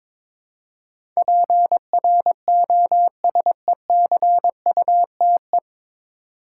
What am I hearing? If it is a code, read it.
Morse code, heard as a keyed tone, about 23 wpm, 700 Hz. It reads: PROSECUTE
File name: 9 Letter Words - First 3 Letters Given - 22wpm